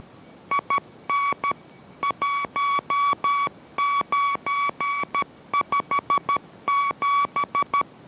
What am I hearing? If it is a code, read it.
Morse code, heard as a keyed tone, about 13 wpm, 1140 Hz. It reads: IN 1957